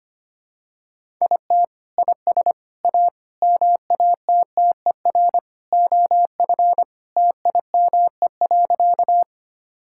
Morse code, heard as a keyed tone, about 25 wpm, 700 Hz. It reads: IT IS A MATTER OF TIME.
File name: practice / 25wpm / Sentences with 3-Word Phrases - 25wpm